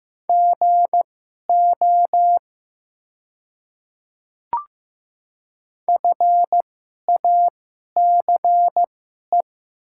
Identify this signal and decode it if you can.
Morse code, heard as a keyed tone, about 15 wpm, spaced as for 11 wpm, 700 Hz. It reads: GO FACE
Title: 2x Character Spacing - Top 300 Words - 15wpm